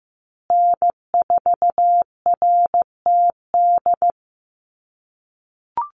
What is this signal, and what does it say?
Morse code, heard as a keyed tone, about 15 wpm, 700 Hz. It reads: N4RTD
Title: USA Call Signs Repeated 3x 15wpm